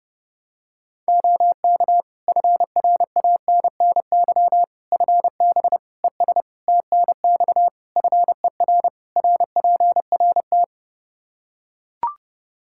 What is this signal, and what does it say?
Morse code, heard as a keyed tone, about 30 wpm, 700 Hz. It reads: OK FRANNY FB ES TNX FER RPRT
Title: Practice QSO - Exchange 2 - Info - 30wpm